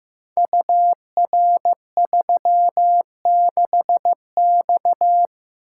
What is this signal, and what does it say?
Morse code, heard as a keyed tone, about 15 wpm, 700 Hz. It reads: UR36X
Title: Code Groups - 5 Letter:Numbers - 15wpm